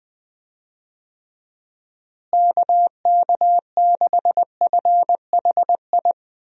Morse code, heard as a keyed tone, about 20 wpm, 700 Hz. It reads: KK6FHI
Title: USA Call Signs - 6 Characters - Repeated 2x - 20wpm